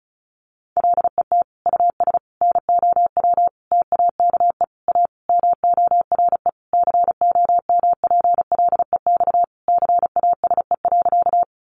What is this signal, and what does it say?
Morse code, heard as a keyed tone, about 34 wpm, 700 Hz. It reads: LET US NOW TAKE A MORE COMPLEX CASE.
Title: Sentences from Top 900 Words 35wpm